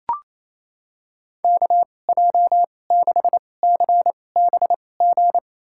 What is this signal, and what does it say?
Morse code, heard as a keyed tone, about 28 wpm, spaced as for 20 wpm, 700 Hz. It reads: KJ6CBG
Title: USA Call Signs - 2x Character Spacing - 28wpm